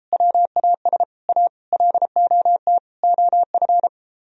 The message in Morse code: WAS A LOT OF

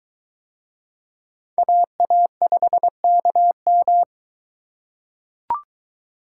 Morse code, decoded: AA5KM

